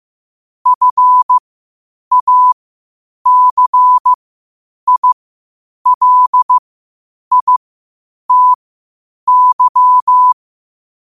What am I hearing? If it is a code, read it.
Morse code, heard as a keyed tone, about 15 wpm, spaced as for 8 wpm, 1000 Hz. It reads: FACILITY
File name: qotdfiles / 8wpm